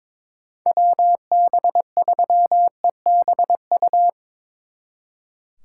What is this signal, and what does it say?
Morse code, heard as a keyed tone, about 24 wpm, 700 Hz. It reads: WB3EBU
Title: Speed Racing - USA 2x3 Call Signs 22wpm